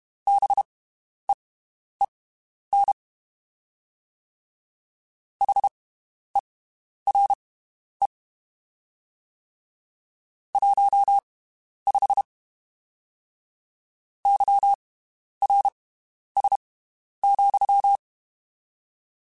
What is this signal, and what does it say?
Morse code, heard as a keyed tone, about 32 wpm, spaced as for 11 wpm, 800 Hz. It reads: BEEN HERE 15 YRS,
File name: (spaced) / QSOs Spaced-mp3s / WD0BGP 1 @30 sp